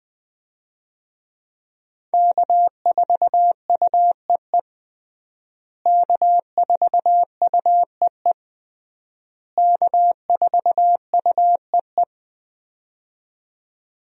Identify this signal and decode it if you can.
Morse code, heard as a keyed tone, about 20 wpm, 700 Hz. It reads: K4UEE K4UEE K4UEE
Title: USA Call Signs - 5 Characters - Repeated 3x - 20wpm